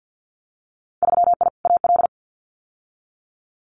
Morse code, heard as a keyed tone, about 50 wpm, 700 Hz. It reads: 3I AL
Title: Winter Field Day Exchanges 50wpm